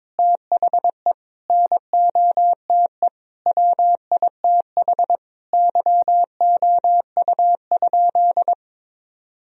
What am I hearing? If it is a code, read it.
Morse code, heard as a keyed tone, about 22 wpm, 700 Hz. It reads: THE NOTE WITH YOU?
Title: Sentences from Top 1000 Words - Review Part 2 - 22wpm